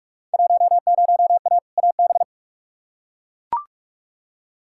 Morse code, heard as a keyed tone, about 42 wpm, 700 Hz. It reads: 10A AB
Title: Field Day Exchanges Encore - 45wpm